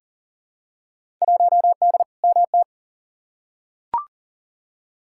Morse code, heard as a keyed tone, about 40 wpm, 700 Hz. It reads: 1D MT
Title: Field Day Exchanges - 40wpm